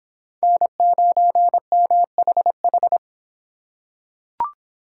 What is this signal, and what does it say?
Morse code, heard as a keyed tone, about 26 wpm, 700 Hz. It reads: N9MHH